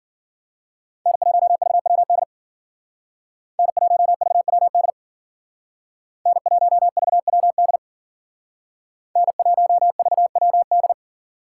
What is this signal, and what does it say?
Morse code, heard as a keyed tone, about 49 wpm, 700 Hz. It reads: N1VWD N1VWD N1VWD N1VWD
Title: Speed Racing - USA Call Signs 40wpm